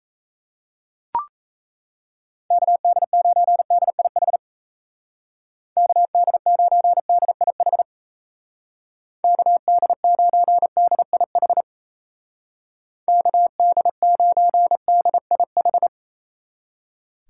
Morse code, heard as a keyed tone, about 37 wpm, 700 Hz. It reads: KD9DIH KD9DIH KD9DIH KD9DIH